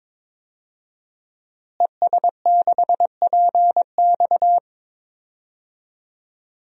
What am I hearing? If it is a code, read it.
Morse code, heard as a keyed tone, about 22 wpm, 700 Hz. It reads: ES6PX
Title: DXCC-100 Call Signs - 5 Characters - Repeated 3x - 22wpm